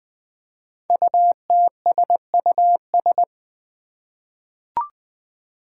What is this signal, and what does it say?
Morse code, heard as a keyed tone, about 20 wpm, 700 Hz. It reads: UTSUS
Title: Five Challenging Letter-Number Combinations 20wpm